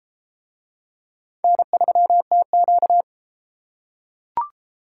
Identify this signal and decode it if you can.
Morse code, heard as a keyed tone, about 33 wpm, 700 Hz. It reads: N3TQ